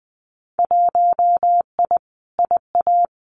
Morse code, heard as a keyed tone, about 20 wpm, 700 Hz. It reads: 1I IA
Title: Winter Field Day Exchanges 20wpm